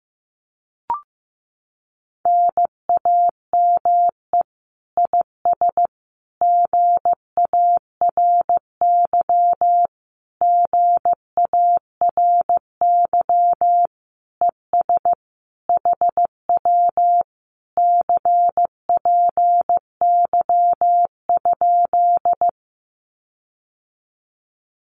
Morse code, heard as a keyed tone, about 15 wpm, 700 Hz. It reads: NAME IS GARY GARY ES HW CPY?
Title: Common QSO Phrases Encore 15wpm